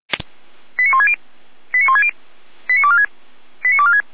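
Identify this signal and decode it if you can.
Five-tone selective call (ZVEI), heard as quick runs of five tones, tones of about 70 ms. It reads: 89160 89160 89246 89246
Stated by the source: ZVEI five-tone call received on an SDR, from the Feuerwehr Orlamünde fire brigade.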